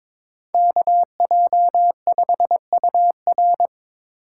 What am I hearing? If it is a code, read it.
Morse code, heard as a keyed tone, about 24 wpm, 700 Hz. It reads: KJ5UR
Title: Speed Racing - USA 2x2 Call Signs 22wpm